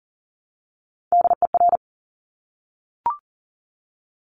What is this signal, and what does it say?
Morse code, heard as a keyed tone, about 39 wpm, 690 Hz. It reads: DER
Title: Top Three Letter Combinations in Words 40wpm